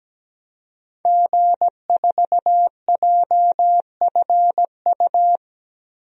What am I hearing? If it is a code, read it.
Morse code, heard as a keyed tone, about 17 wpm, 700 Hz. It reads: G4JFU
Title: DXCC-100 Call Signs - 5 Characters - 17wpm